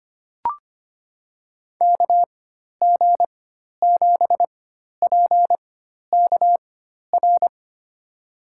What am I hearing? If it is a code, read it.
Morse code, heard as a keyed tone, about 25 wpm, spaced as for 12 wpm, 700 Hz. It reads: KG7PKR
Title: USA Call Signs - 6 Characters - 4x Character Spacing - 25wpm